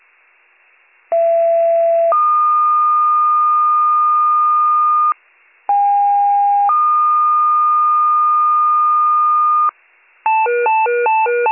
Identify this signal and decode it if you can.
Two-tone page, heard as a one-second tone, then a three-second tone, about 670 then 1220 Hz, about 810 then 1220 Hz. It is heard from 1.1 to 9.7 s.